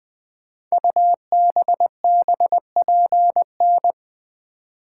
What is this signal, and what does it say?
Morse code, heard as a keyed tone, about 20 wpm, 700 Hz. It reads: UBBPN
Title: Code Groups - 5 Letter:Numbers - 20wpm